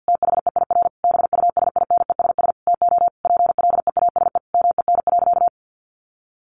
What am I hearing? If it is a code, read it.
Morse code, heard as a keyed tone, about 48 wpm, 700 Hz. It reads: THEIR BUSINESS TO PLEASE MEN.
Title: Sentences from Top 1500 Words 50wpm